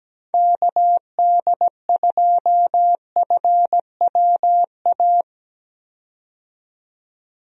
Morse code, heard as a keyed tone, about 17 wpm, 700 Hz. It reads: KD2FWA
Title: USA Call Signs - 6 Characters - 17wpm